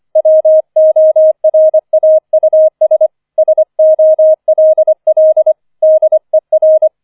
Morse code, heard as a keyed tone, about 25 wpm, 630 Hz. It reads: WORAUS SOLL DER
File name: 004 Vierter Streich 25wpm